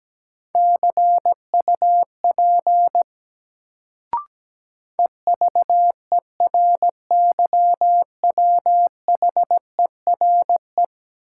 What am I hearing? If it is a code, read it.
Morse code, heard as a keyed tone, about 17 wpm, 700 Hz. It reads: CUP EVERYWHERE
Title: Sentences from Top 1600 Words - New Word Drill Alt - 17wpm